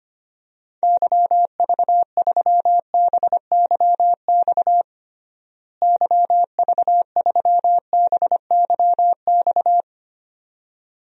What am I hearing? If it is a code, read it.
Morse code, heard as a keyed tone, about 25 wpm, 700 Hz. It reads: YV3BYX YV3BYX
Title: DXCC-100 Call Signs - 6 Characters - Repeated 2x - 25wpm